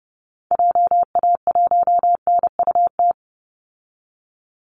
Morse code, heard as a keyed tone, about 30 wpm, 700 Hz. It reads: JA1NUT
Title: Call Signs Repeated 2x - 30wpm